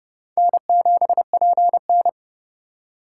Morse code, heard as a keyed tone, about 30 wpm, 700 Hz. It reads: N7PN
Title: USA Call Signs - 4 Characters - 8x Character Spacing - 30wpm